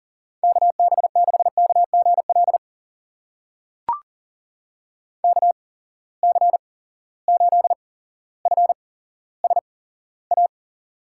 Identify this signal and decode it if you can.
Morse code, heard as a keyed tone, about 40 wpm, spaced as for 17 wpm, 700 Hz. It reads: KB6KGL KC8FSA
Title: USA Call Signs - 6 Characters - 8x Character Spacing - 40wpm